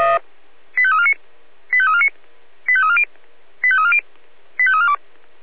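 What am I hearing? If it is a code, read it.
Five-tone selective call (ZVEI), heard as quick runs of five tones, tones of about 70 ms. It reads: 85349 85349 85340 85340 85342
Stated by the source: ZVEI five-tone selective call received by SDR.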